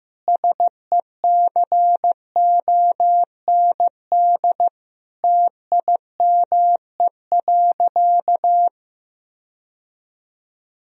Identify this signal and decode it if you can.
Morse code, heard as a keyed tone, about 16 wpm, 700 Hz. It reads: SECOND TIME.
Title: Sentences from Top 1000 Words - Review Part 1 - 15wpm